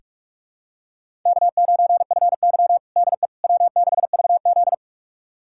Morse code, heard as a keyed tone, about 42 wpm, 700 Hz. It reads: K9FY DE W6VZ
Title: USA Callsign DE Callsign - 4 Characters - 45wpm